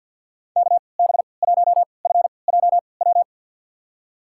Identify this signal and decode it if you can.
Morse code, heard as a keyed tone, about 49 wpm, spaced as for 28 wpm, 700 Hz. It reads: KB1FJW
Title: USA Call Signs - 3x Character Spacing - 50wpm